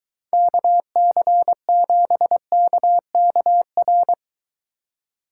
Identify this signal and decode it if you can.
Morse code, heard as a keyed tone, about 23 wpm, 700 Hz. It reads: KC7KKR